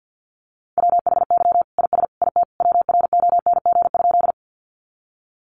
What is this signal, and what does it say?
Morse code, heard as a keyed tone, about 50 wpm, 700 Hz. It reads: WHY IS IT WRONG?
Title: Sentences from Top 900 Words 50wpm